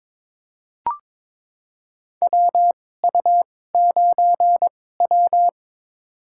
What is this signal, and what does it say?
Morse code, heard as a keyed tone, about 22 wpm, spaced as for 16 wpm, 700 Hz. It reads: WU9W
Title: USA Call Signs - 4 Characters - 2x Character Spacing - 22wpm